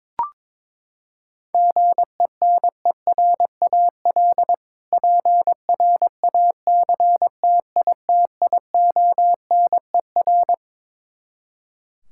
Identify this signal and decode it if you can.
Morse code, heard as a keyed tone, about 22 wpm, 700 Hz. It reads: GENERAL PRACTITIONER
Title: K1USN SST QSO Practice 22wpm